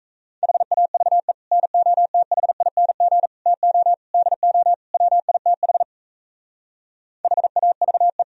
Morse code, heard as a keyed tone, about 36 wpm, 700 Hz. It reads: HAVE NOTHING TO DO WITH HAVE